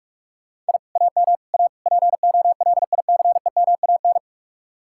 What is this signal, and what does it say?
Morse code, heard as a keyed tone, about 59 wpm, spaced as for 51 wpm, 700 Hz. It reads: I AM A POLICEMAN
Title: Speed-Racing Common QSO Phrases - 45wpm